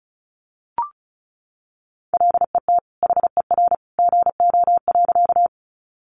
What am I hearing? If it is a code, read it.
Morse code, heard as a keyed tone, about 34 wpm, 700 Hz. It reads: LET HER GO.
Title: Sentences from Top 300 Words 35wpm